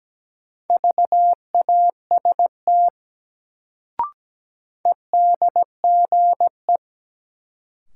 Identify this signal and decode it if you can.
Morse code, heard as a keyed tone, about 17 wpm, 700 Hz. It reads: VAST EDGE
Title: Sentences from Top 1300 Words - New Word Drill - 17wpm